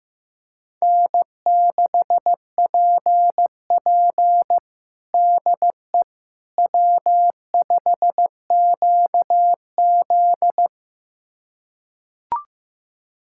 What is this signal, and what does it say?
Morse code, heard as a keyed tone, about 15 wpm, 700 Hz. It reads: N6PP DE W5QZ
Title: USA Callsign DE Callsign - 4 Characters - 15wpm